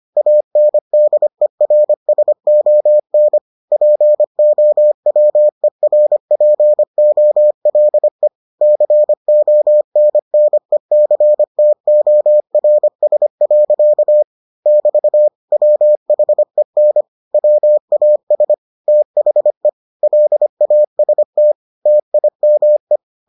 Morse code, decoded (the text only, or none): ANDERSON POWERPOLE CONNECTORS. <BT> WHEN WAS THE LAST TIME